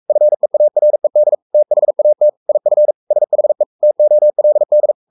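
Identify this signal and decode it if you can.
Morse code, heard as a keyed tone, about 44 wpm, 600 Hz. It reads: FEARED THAT IF SHE TOLD
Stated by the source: Morse code at 43 wpm sending Ken Follett's novel Never in English, from CW med Gnister.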